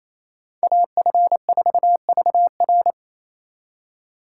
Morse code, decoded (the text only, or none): AF4VR